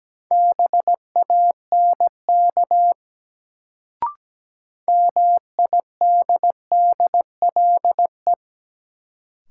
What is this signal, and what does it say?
Morse code, heard as a keyed tone, about 17 wpm, 700 Hz. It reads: BANK MIDDLE
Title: Sentences from Top 900 Words - New Word Drill - 17wpm